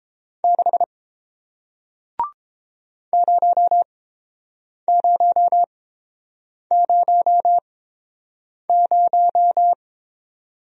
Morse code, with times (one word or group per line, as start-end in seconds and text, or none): 0.4-0.8 6
3.1-9.7 0000